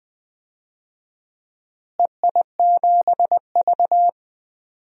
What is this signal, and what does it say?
Morse code, heard as a keyed tone, about 20 wpm, 700 Hz. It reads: EI7V
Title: DXCC-100 Call Signs - 4 Characters - Repeated 2x - 20wpm